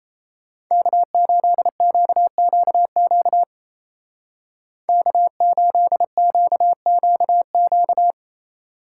Morse code, heard as a keyed tone, about 32 wpm, 700 Hz. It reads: K8QQQ K8QQQ
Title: Speed Racing - USA 1x3 Call Signs 28wpm